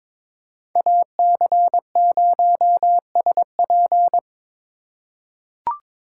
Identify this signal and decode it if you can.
Morse code, heard as a keyed tone, about 22 wpm, 700 Hz. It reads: AC0SP
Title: USA Call Signs - 2x Character Spacing - 22wpm